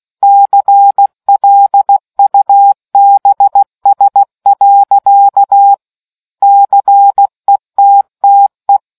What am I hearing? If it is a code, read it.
Morse code, heard as a keyed tone, about 16 wpm, 800 Hz. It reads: CLUBS. CETTE